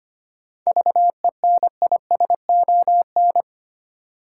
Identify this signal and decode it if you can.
Morse code, heard as a keyed tone, about 25 wpm, 700 Hz. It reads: VENISON
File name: practice / 25wpm / Difficult Words with Hints - 25wpm